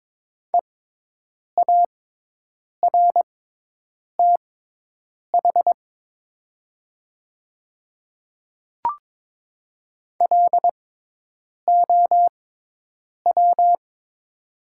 Morse code, heard as a keyed tone, about 22 wpm, spaced as for 8 wpm, 700 Hz. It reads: EARTH LOW
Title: 6x Character Spacing - Top 200 Words - 22wpm